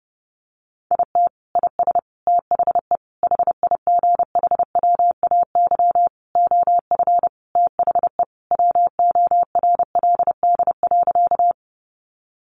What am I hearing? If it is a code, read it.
Morse code, heard as a keyed tone, about 30 wpm, 700 Hz. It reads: IT IS THE HIGHWAY OF THE WORLD.
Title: Sentences from Top 1400 Words 30wpm